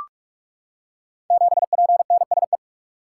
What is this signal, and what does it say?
Morse code, heard as a keyed tone, about 44 wpm, 700 Hz. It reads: 7PNSE
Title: Code Groups - 5 Letter:Numbers - 45wpm